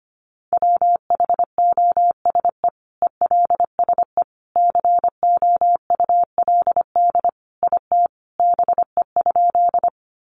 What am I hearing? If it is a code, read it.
Morse code, heard as a keyed tone, about 25 wpm, 700 Hz. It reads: WHOSE ELSE COULD IT BE?
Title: Sentences from Top 1200 Words 25wpm